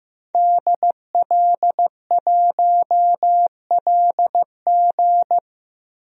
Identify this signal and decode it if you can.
Morse code, heard as a keyed tone, about 15 wpm, 700 Hz. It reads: DL1LG
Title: DXCC-100 Call Signs - 5 Characters - 15wpm